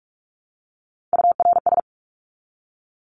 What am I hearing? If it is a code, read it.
Morse code, heard as a keyed tone, about 44 wpm, 700 Hz. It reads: URS
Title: Abbreviations Repeated 3x - 45wpm